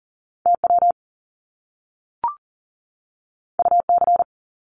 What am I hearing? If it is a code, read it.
Morse code, heard as a keyed tone, about 38 wpm, 700 Hz. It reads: TW UC
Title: Top Two Letter Combinations in Words 40wpm